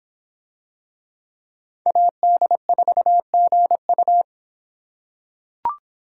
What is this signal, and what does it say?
Morse code, heard as a keyed tone, about 26 wpm, 700 Hz. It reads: AD4GU